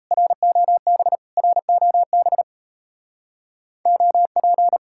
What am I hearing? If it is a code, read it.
Morse code, heard as a keyed tone, about 34 wpm, 700 Hz. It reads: ROB ROB OP
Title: Speed-Racing Common QSO Phrases - 28wpm